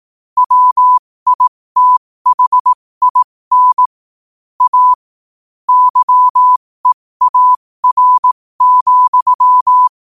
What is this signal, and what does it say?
Morse code, heard as a keyed tone, about 18 wpm, spaced as for 15 wpm, 1000 Hz. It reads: WITHIN A YEAR,
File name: qotdfiles / 15wpm